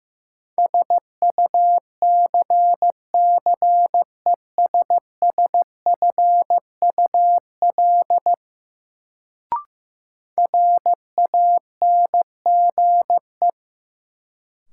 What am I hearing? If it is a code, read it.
Morse code, heard as a keyed tone, about 15 wpm, 700 Hz. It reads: SUCCESSFUL RANGE
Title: Sentences from Top 900 Words - New Word Drill - 15wpm